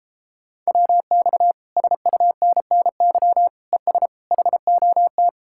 Morse code, heard as a keyed tone, about 33 wpm, 700 Hz. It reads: WX SUNNY ES HOT